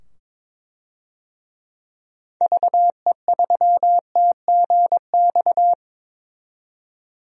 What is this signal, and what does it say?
Morse code, heard as a keyed tone, about 22 wpm, 700 Hz. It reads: VE3TGX
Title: DXCC-100 Call Signs - 6 Characters - 22wpm